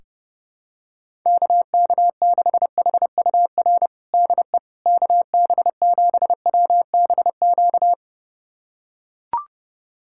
Morse code, heard as a keyed tone, about 30 wpm, 700 Hz. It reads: KK6HUR DE KB7WBQ
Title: USA Callsign DE Callsign - 6 Characters - 30wpm